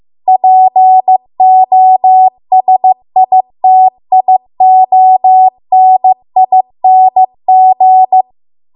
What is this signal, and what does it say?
Morse code, heard as a keyed tone, about 15 wpm, 750 Hz. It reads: POSITIONING
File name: Archive / 15 WPM / 15WPM